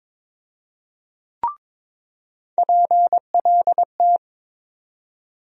Morse code, heard as a keyed tone, about 21 wpm, 700 Hz. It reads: PLT
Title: USA Call Sign Postfixes - 3 Letters 22wpm